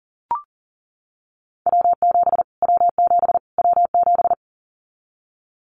Morse code, heard as a keyed tone, about 40 wpm, 700 Hz. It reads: W7 W7 W7
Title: Callsign Prefixes for Top 25 DXCC - Repeated 3x - 40wpm